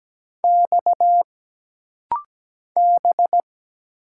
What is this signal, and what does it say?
Morse code, heard as a keyed tone, about 17 wpm, 700 Hz. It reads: X B
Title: Single Letter-Number - Rapid-Fire 17wpm